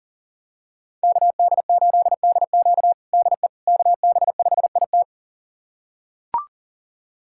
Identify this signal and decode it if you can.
Morse code, heard as a keyed tone, about 40 wpm, 700 Hz. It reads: KD8DQ DE KB5IT
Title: USA Callsign DE Callsign - 5 Characters - 40wpm